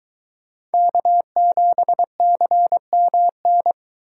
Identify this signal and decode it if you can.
Morse code, heard as a keyed tone, about 22 wpm, 700 Hz. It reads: K7CMN